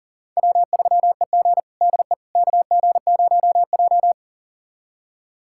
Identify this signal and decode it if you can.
Morse code, heard as a keyed tone, about 40 wpm, 700 Hz. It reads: W3EG DE KG0J
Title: USA Callsign DE Callsign - 4 Characters - 40wpm